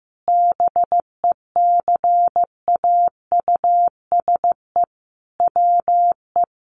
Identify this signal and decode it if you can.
Morse code, heard as a keyed tone, about 15 wpm, 700 Hz. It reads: BECAUSE WE
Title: Sentences from Top 1400 Words 15wpm